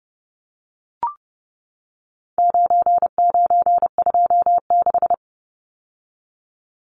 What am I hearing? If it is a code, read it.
Morse code, heard as a keyed tone, about 30 wpm, 700 Hz. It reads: 9926
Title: Numbers - 4-Digit 30wpm